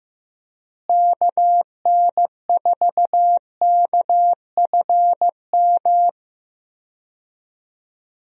Morse code, heard as a keyed tone, about 15 wpm, 700 Hz. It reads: KN4KFM
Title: USA Call Signs - 6 Characters - Repeated 3x - 15wpm